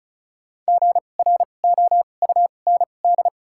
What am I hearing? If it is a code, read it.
Morse code, heard as a keyed tone, about 34 wpm, spaced as for 25 wpm, 700 Hz. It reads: GROUND
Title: 2x Character Spacing - Top 500 Words - 35wpm